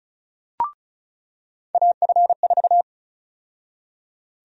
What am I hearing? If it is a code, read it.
Morse code, heard as a keyed tone, about 36 wpm, 700 Hz. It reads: AF4
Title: USA Call Sign Prefixes - 35wpm